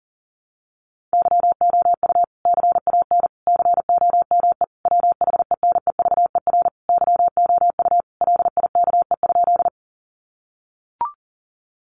Morse code, heard as a keyed tone, about 40 wpm, 700 Hz. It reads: YOU CAN COME WHENEVER YOU LIKE?
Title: Sentences from Top 1200 Words 40wpm